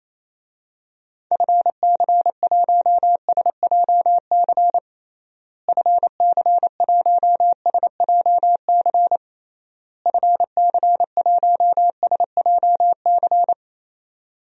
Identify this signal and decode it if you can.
Morse code, heard as a keyed tone, about 28 wpm, 700 Hz. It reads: FC1SJC FC1SJC FC1SJC
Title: DXCC-100 Call Signs - 6 Characters - Repeated 3x - 28wpm